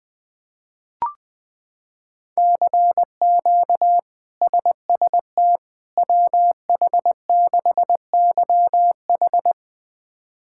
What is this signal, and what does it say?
Morse code, heard as a keyed tone, about 20 wpm, 700 Hz. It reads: CQ SST WH6YH
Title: K1USN SST QSO Practice - 20wpm